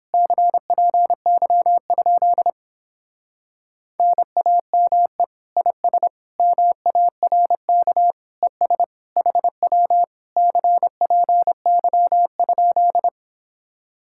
Morse code, heard as a keyed tone, about 29 wpm, 700 Hz. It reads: CPY? NAME IS MARK ES HW CPY?